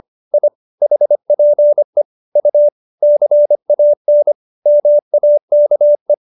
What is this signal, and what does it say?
Morse code, heard as a keyed tone, about 25 wpm, 590 Hz. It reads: I HPE U CAN MAKE